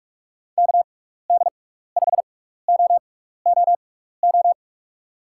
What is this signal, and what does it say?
Morse code, heard as a keyed tone, about 41 wpm, spaced as for 15 wpm, 700 Hz. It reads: KD5OOO
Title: USA Call Signs - 6 Characters - 6x Character Spacing - 45wpm